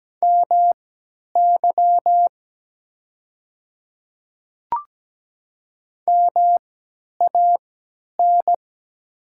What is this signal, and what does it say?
Morse code, heard as a keyed tone, about 17 wpm, spaced as for 10 wpm, 700 Hz. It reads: MY MAN
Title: 3x Character Spacing - Top 500 Words - 17wpm